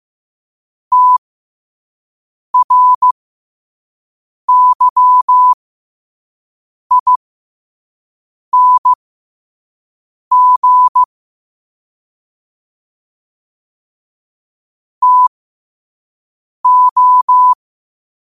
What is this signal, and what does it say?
Morse code, heard as a keyed tone, about 15 wpm, spaced as for 5 wpm, 1000 Hz. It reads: TRYING TO
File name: qotdfiles / 5wpm